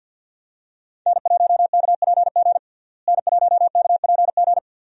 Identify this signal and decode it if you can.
Morse code, heard as a keyed tone, about 48 wpm, 700 Hz. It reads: N1KPG N1KPG
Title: USA Call Signs - 5 Characters - Repeated 2x - 50wpm